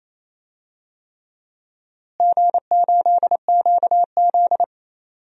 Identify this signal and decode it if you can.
Morse code, heard as a keyed tone, about 28 wpm, 700 Hz. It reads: G8QZ
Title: DXCC-100 Call Signs - 4 Characters - Repeated 3x - 28wpm